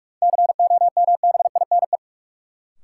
Morse code, heard as a keyed tone, about 46 wpm, 700 Hz. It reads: COMBINE
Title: Sentences from Top 1600 Words - New Word Drill - 45wpm